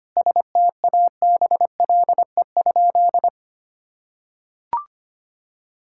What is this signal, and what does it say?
Morse code, heard as a keyed tone, about 25 wpm, 700 Hz. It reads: STABLE?
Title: Sentences from Top 1600 Words 25wpm